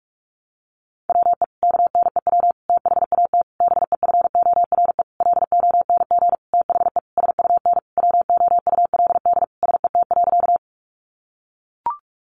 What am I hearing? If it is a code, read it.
Morse code, heard as a keyed tone, about 44 wpm, 700 Hz. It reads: WE KNEW THAT BEFORE LONG THE SUN WOULD SET.
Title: Sentences from Top 400 Words 45wpm